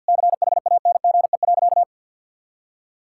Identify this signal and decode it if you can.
Morse code, heard as a keyed tone, about 52 wpm, 700 Hz. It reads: CHANGE.
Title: Sentences from Top 1000 Words - Review Part 6 - 50wpm